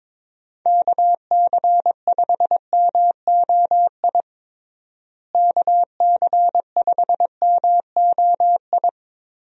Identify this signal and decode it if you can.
Morse code, heard as a keyed tone, about 22 wpm, 700 Hz. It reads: KC5MOI KC5MOI
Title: USA Call Signs - 6 Characters - Repeated 3x - 22wpm